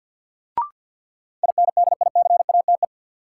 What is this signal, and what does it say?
Morse code, heard as a keyed tone, about 52 wpm, 700 Hz. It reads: INDICATE
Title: Sentences from Top 1300 Words - New Word Drill Alt - 50wpm